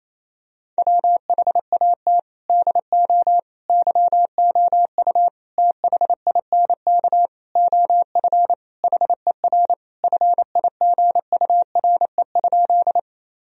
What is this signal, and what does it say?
Morse code, heard as a keyed tone, about 28 wpm, 700 Hz. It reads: WHAT DO YOU THINK OF HER FIGURE?
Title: Sentences from Top 1000 Words - Review Part 2 - 28wpm